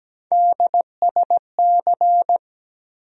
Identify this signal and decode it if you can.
Morse code, heard as a keyed tone, about 17 wpm, 700 Hz. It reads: DSC
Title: USA Call Sign Postfixes - 3 Letters 17wpm